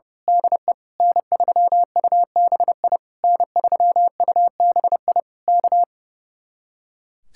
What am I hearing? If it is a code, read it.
Morse code, heard as a keyed tone, about 28 wpm, 700 Hz. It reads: DE N3UBI N3UBI K
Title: Speed-Racing Common QSO Phrases - 30wpm